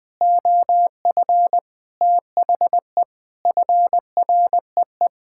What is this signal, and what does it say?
Morse code, heard as a keyed tone, about 20 wpm, 700 Hz. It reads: OF THE FREE
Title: Sets of 6 Words - 20wpm